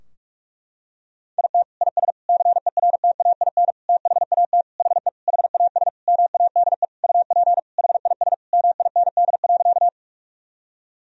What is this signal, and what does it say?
Morse code, heard as a keyed tone, about 46 wpm, 700 Hz. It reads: IT IS CERTAIN THAT HE HAS MADE UP HIS MIND.
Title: Sentences from Top 1000 Words - Review Part 2 - 45wpm